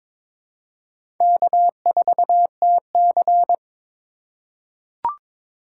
Morse code, heard as a keyed tone, about 21 wpm, 700 Hz. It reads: K4TC
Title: USA Call Signs - 4 Characters - 3x Character Spacing - 22wpm